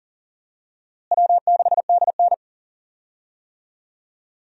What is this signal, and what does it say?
Morse code, heard as a keyed tone, about 37 wpm, 700 Hz. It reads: W6DN
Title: USA Call Signs - 4 Characters - Repeated 2x - 40wpm